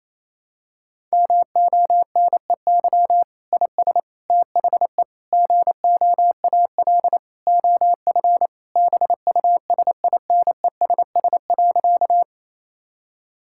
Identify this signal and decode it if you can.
Morse code, heard as a keyed tone, about 28 wpm, 700 Hz. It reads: MONEY IS THE GOAL OF BUSINESS.
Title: Sentences from Top 1000 Words - Review Part 5 - 28wpm